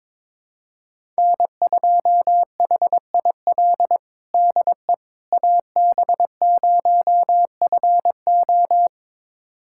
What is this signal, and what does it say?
Morse code, heard as a keyed tone, about 22 wpm, 700 Hz. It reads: N2HIL DE AB0FO
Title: USA Callsign DE Callsign - 5 Characters - 22wpm